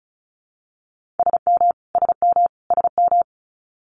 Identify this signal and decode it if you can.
Morse code, heard as a keyed tone, about 36 wpm, 700 Hz. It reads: SM SM SM
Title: Multi-letter Phonemes Repeated 3x 35wpm